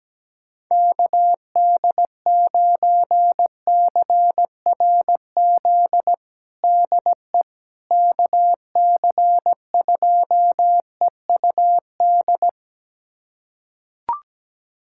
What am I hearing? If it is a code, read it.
Morse code, heard as a keyed tone, about 17 wpm, 700 Hz. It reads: KD9CRZ DE KC2EUD
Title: USA Callsign DE Callsign 17wpm